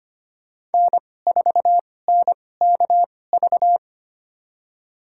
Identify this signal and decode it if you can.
Morse code, heard as a keyed tone, about 25 wpm, spaced as for 18 wpm, 700 Hz. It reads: N4NKV
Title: USA Call Signs - 5 Characters - 2x Character Spacing - 25wpm